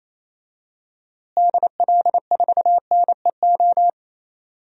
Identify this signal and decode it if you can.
Morse code, heard as a keyed tone, about 28 wpm, 700 Hz. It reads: DL4NEO
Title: DXCC-100 Call Signs - 6 Characters - Repeated 2x - 28wpm